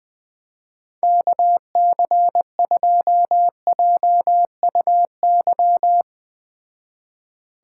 Morse code, heard as a keyed tone, about 20 wpm, 700 Hz. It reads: KC2JUY